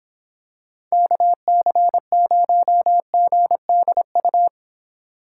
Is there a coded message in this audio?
Yes: Morse code from 0.9 to 4.5 s, heard as a keyed tone, about 26 wpm, 700 Hz. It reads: KC0GDU